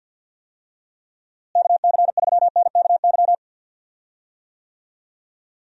Morse code, heard as a keyed tone, about 51 wpm, 700 Hz. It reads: KC3NKY
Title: USA Call Signs - 6 Characters - Repeated 3x - 50wpm